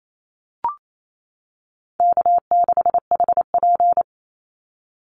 Morse code, heard as a keyed tone, about 28 wpm, 700 Hz. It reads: K6HP
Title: Easy Call Signs - 28wpm